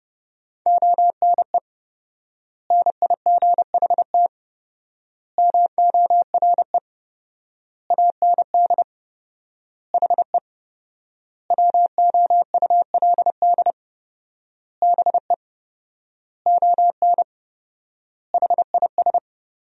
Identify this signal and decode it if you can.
Morse code, heard as a keyed tone, about 30 wpm, 700 Hz. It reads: ONE NIGHT MORE AND HE WOULD BE ON HIS
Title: 4x Word-Spacing - Sentences from Top 300 Words - 30wpm